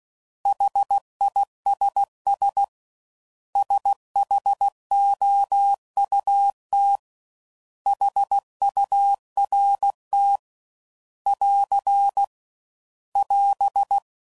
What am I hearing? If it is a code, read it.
Morse code, heard as a keyed tone, about 16 wpm, 800 Hz. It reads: HISS SHOUT HURT <AR> <AS>
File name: Prefixes / By First Letter / S vs H